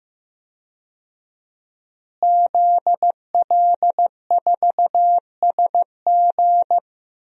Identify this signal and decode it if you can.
Morse code, heard as a keyed tone, about 15 wpm, 700 Hz. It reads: ZL4SG
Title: DXCC-100 Call Signs - 5 Characters - Repeated 3x - 15wpm